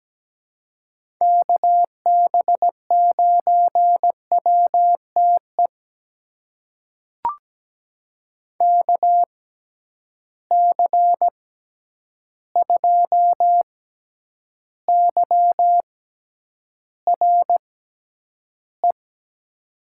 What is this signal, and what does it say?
Morse code, heard as a keyed tone, about 17 wpm, spaced as for 9 wpm, 700 Hz. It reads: KB9WTE KC2YRE
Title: USA Call Signs - 6 Characters - 6x Character Spacing - 17wpm